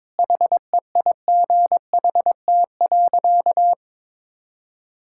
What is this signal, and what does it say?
Morse code, heard as a keyed tone, about 21 wpm, 700 Hz. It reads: HEIGHT.
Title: Sentences from Top 1600 Words 22wpm